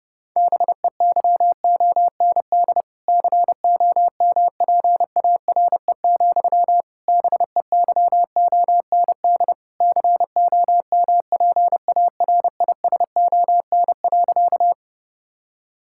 Morse code, heard as a keyed tone, about 32 wpm, 700 Hz. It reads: BEYOND COMPARE, BEYOND COMPARISON.
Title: Sentences from Top 1600 Words 30wpm